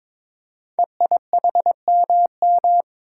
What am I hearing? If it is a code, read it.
Morse code, heard as a keyed tone, about 22 wpm, 690 Hz. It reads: EIHMM
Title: Code Groups - 5 Letters - 22wpm